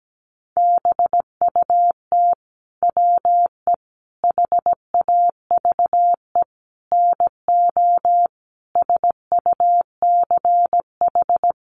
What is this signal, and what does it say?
Morse code, heard as a keyed tone, about 17 wpm, 700 Hz. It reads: BUT WE HAVE NO SUCH
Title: Sentences from Top 1300 Words 17wpm